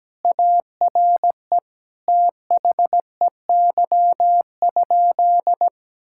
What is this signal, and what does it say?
Morse code, heard as a keyed tone, about 17 wpm, 700 Hz. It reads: ARE THEY?
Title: Sentences from Top 1000 Words - Review Part 5 - 17wpm